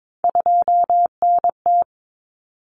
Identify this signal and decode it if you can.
Morse code, heard as a keyed tone, about 22 wpm, 700 Hz. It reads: 2NT
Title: Common Cut Numbers - 3-Digit 22wpm